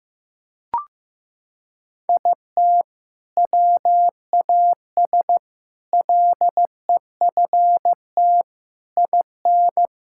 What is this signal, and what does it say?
Morse code, heard as a keyed tone, about 15 wpm, 700 Hz. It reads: IT WAS LEFT IN
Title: Sentences from Top 1000 Words - Review Part 6 - 15wpm